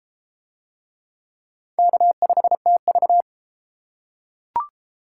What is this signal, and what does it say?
Morse code, heard as a keyed tone, about 28 wpm, 700 Hz. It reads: K5TV